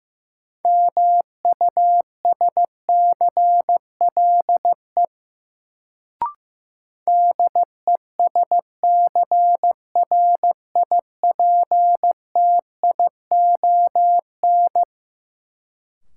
Morse code, heard as a keyed tone, about 15 wpm, 700 Hz. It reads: MUSCLE DESCRIPTION
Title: Sentences from Top 1300 Words - New Word Drill Alt - 15wpm